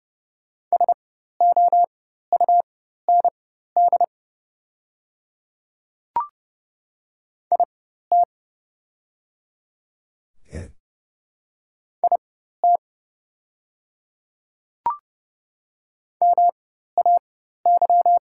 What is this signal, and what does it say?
Morse code, heard as a keyed tone, about 30 wpm, spaced as for 14 wpm, 700 Hz. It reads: SOUND IT IT MAY
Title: 4x Character Spacing - Top 100 Words - 30wpm